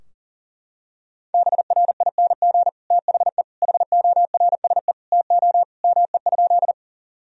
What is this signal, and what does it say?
Morse code, heard as a keyed tone, about 40 wpm, 700 Hz. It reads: BRING THE HORSE TO ME?
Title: Sentences from Top 1000 Words - Review Part 3 - 40wpm